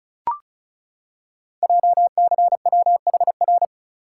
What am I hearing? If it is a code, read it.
Morse code, heard as a keyed tone, about 36 wpm, 690 Hz. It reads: JCWHR
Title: Code Groups - 5 Letters - 35wpm